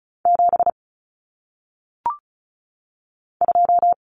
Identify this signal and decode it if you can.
Morse code, heard as a keyed tone, about 36 wpm, 700 Hz. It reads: 7 2
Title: Common Cut Numbers - 1-Digit 35wpm